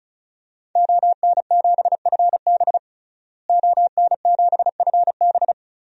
Morse code, heard as a keyed tone, about 36 wpm, 700 Hz. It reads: ON7FB ON7FB
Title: DXCC-100 Call Signs - 5 Characters - Repeated 3x - 35wpm